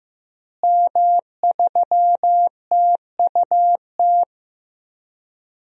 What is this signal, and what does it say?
Morse code, heard as a keyed tone, about 15 wpm, 700 Hz. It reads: M3TUT
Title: DXCC-100 Call Signs - 5 Characters - Repeated 2x - 15wpm